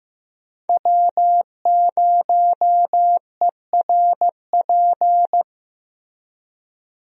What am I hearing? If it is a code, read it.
Morse code, heard as a keyed tone, about 15 wpm, 700 Hz. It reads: W0ERP
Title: USA Call Signs - 5 Characters - Repeated 2x - 15wpm